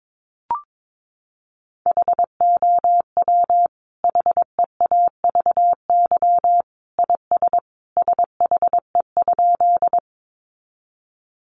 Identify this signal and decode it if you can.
Morse code, heard as a keyed tone, about 22 wpm, 700 Hz. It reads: HOW HEAVY IS SHE?
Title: Sentences from Top 1100 Words 22wpm